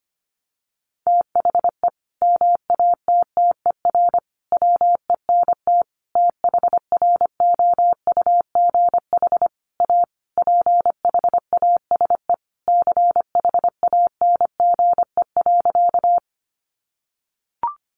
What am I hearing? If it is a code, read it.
Morse code, heard as a keyed tone, about 25 wpm, 700 Hz. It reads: THE MATTER WENT THROUGH A PHASE CHANGE.
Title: Sentences from Top 1200 Words 25wpm